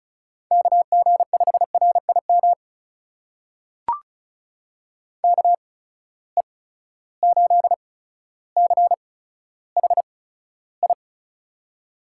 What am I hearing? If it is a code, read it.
Morse code, heard as a keyed tone, about 36 wpm, spaced as for 15 wpm, 700 Hz. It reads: KG5RIM KE8CHI
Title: USA Call Signs - 6 Characters - 8x Character Spacing - 35wpm